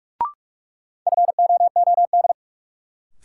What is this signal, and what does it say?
Morse code, heard as a keyed tone, about 46 wpm, 700 Hz. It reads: FOOD
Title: Sets of 5 Words - New Word Drill - 45wpm